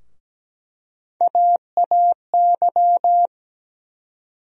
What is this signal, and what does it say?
Morse code, heard as a keyed tone, about 17 wpm, 700 Hz. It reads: AAY
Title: USA Call Sign Postfixes - 3 Letters 17wpm